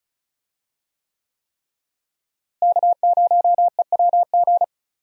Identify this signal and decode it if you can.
Morse code, heard as a keyed tone, about 34 wpm, 700 Hz. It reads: K0EWG
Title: USA Call Signs - 5 Characters - Repeated 3x - 35wpm